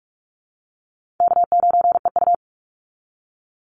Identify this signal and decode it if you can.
Morse code, heard as a keyed tone, about 44 wpm, 700 Hz. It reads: K9EU
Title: Easy Call Signs Repeated 2x - 45wpm